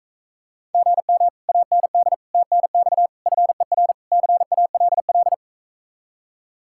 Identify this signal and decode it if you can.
Morse code, heard as a keyed tone, about 45 wpm, 700 Hz. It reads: GM AND TNX FER CALL